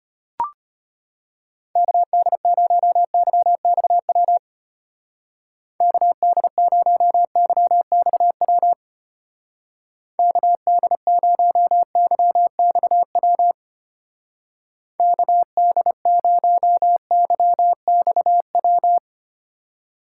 Morse code, heard as a keyed tone, about 32 wpm, 700 Hz. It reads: KD0YXW KD0YXW KD0YXW KD0YXW